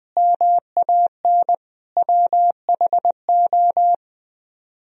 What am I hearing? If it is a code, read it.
Morse code, heard as a keyed tone, about 20 wpm, 700 Hz. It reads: MAN WHO